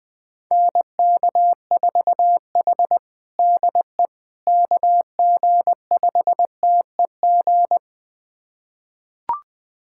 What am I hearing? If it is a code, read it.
Morse code, heard as a keyed tone, about 20 wpm, 700 Hz. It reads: NK4H DE KG5TEG
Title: USA Callsign DE Callsign 20wpm